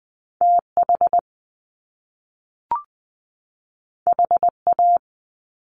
Morse code, heard as a keyed tone, about 20 wpm, 700 Hz. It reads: TH HA
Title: Top Two Letter Combinations in Words 20wpm